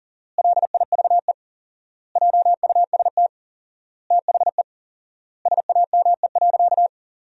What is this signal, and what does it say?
Morse code, heard as a keyed tone, about 37 wpm, 700 Hz. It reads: LIVE JUST THE SAME.
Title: 4x Word-Spacing - Sentences from Top 300 Words - 40wpm